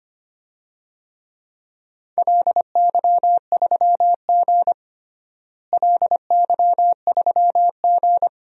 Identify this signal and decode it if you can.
Morse code, heard as a keyed tone, about 25 wpm, 700 Hz. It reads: LY3G LY3G
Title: DXCC-100 Call Signs - 4 Characters - Repeated 2x - 25wpm